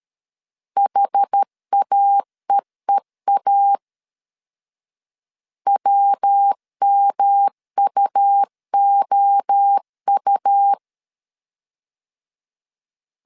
Morse code, heard as a keyed tone, about 13 wpm, 800 Hz. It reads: HAEEA WMUOU
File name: CW / DET / 1 3 1500C S B 12wpm 2828